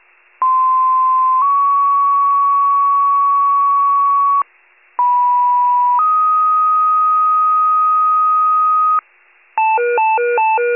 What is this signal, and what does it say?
Two-tone page, heard as a one-second tone, then a three-second tone, about 1040 then 1120 Hz, about 980 then 1250 Hz. It is heard from 0.4 to 9.0 s.